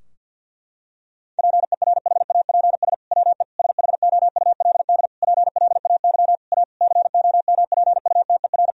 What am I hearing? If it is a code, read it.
Morse code, heard as a keyed tone, about 50 wpm, 700 Hz. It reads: PERHAPS WE SHOULD PLAY A COMPUTER
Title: Sentences from Top 1000 Words - Review Part 3 - 50wpm